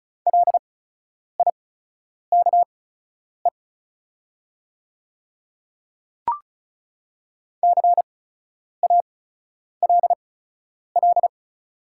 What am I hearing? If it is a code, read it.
Morse code, heard as a keyed tone, about 36 wpm, spaced as for 10 wpm, 700 Hz. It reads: LIKE CALL
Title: 8x Character Spacing - Top 100 Words - 35wpm